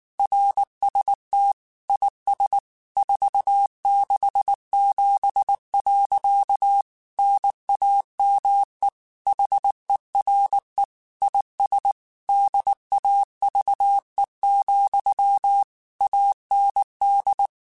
Morse code, heard as a keyed tone, about 19 wpm, 800 Hz. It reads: RST IS 467. NAME HERE IS DAVE, AND
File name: QSOs / QSOs-mp3s / kc7sac@18